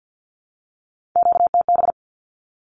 Morse code, heard as a keyed tone, about 50 wpm, 700 Hz. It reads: QTB
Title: Q-Codes 50wpm